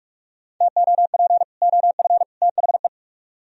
Morse code, heard as a keyed tone, about 49 wpm, 700 Hz. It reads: TOP OF THE